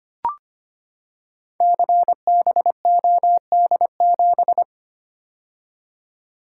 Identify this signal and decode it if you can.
Morse code, heard as a keyed tone, about 25 wpm, 700 Hz. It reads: CBOD7
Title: Code Groups - 5 Letter:Numbers - 25wpm